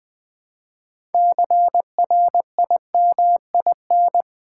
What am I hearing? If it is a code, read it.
Morse code, heard as a keyed tone, about 20 wpm, 700 Hz. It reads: CRIMIN
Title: Build A Word - 3 Letters At A Time - 8-Letter Words - 20wpm